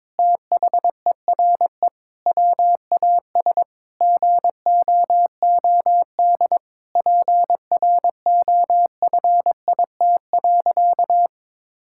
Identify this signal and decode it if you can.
Morse code, heard as a keyed tone, about 22 wpm, 700 Hz. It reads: THERE WAS GOOD PROFIT.
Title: Sentences from Top 1600 Words 22wpm